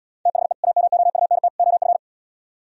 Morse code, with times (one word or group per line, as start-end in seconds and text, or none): 0.3-0.5 THE
0.6-1.5 AMOUNT
1.6-2.0 OF